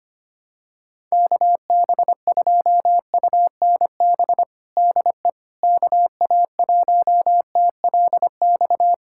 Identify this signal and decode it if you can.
Morse code, heard as a keyed tone, about 25 wpm, 700 Hz. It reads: KB2UNB DE KA1TLX
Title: USA Callsign DE Callsign - 6 Characters - 25wpm